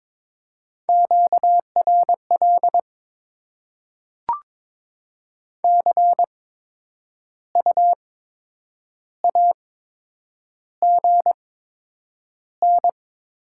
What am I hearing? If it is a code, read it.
Morse code, heard as a keyed tone, about 22 wpm, spaced as for 8 wpm, 700 Hz. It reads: QRL CU AGN
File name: Top 20 QSO Elements - 8x Character Spacing - 22wpm